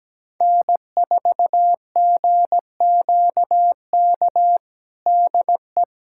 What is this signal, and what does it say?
Morse code, heard as a keyed tone, about 17 wpm, 700 Hz. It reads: N4GQK DE
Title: USA Callsign DE Callsign 17wpm